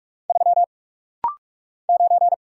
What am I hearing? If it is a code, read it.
Morse code, heard as a keyed tone, about 46 wpm, 700 Hz. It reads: 3 9
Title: Single Letter-Number - Rapid-Fire 45wpm